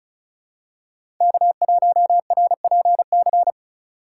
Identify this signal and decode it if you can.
Morse code, heard as a keyed tone, about 36 wpm, 700 Hz. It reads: K1RPC
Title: USA Call Signs - 4x Character Spacing - 35wpm